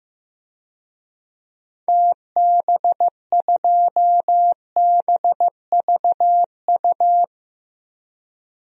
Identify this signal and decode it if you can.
Morse code, heard as a keyed tone, about 15 wpm, 700 Hz. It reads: TB2BVU
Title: DXCC-100 Call Signs - 6 Characters - 15wpm